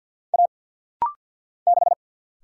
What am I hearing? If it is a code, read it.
Morse code, heard as a keyed tone, about 50 wpm, 700 Hz. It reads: A 6
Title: Single Letter-Number Alt - Rapid-Fire 50wpm